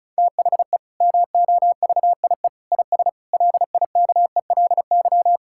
Morse code, heard as a keyed tone, about 36 wpm, 700 Hz. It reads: THE MOVIE IS LIKELY
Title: Sentences from Top 1000 Words - Review Part 5 - 35wpm